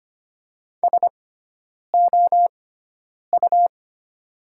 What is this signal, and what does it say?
Morse code, heard as a keyed tone, about 25 wpm, spaced as for 9 wpm, 700 Hz. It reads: SOU
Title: 6x Character Spacing - Top 300 Words - 25wpm